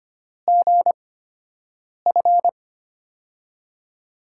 Morse code, heard as a keyed tone, about 25 wpm, spaced as for 7 wpm, 700 Hz. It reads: GF
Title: USA Call Signs - 5 Characters - 8x Character Spacing - 25wpm